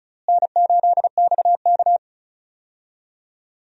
Morse code, heard as a keyed tone, about 34 wpm, 700 Hz. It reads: N8XK
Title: USA Call Signs - 4 Characters - Repeated 2x - 35wpm